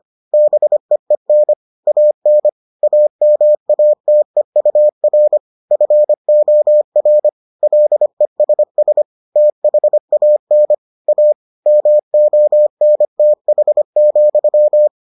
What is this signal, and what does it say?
Morse code, heard as a keyed tone, about 25 wpm, 600 Hz. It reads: BEEN AN AMATEUR FOR LESS THAN A MONTH,